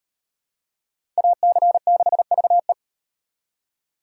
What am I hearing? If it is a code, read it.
Morse code, heard as a keyed tone, about 37 wpm, 700 Hz. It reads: AC6VE